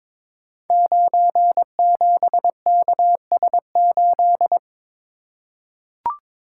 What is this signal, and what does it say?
Morse code, heard as a keyed tone, about 22 wpm, 700 Hz. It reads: 97KS8
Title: Code Groups - 5 Letter:Numbers - 22wpm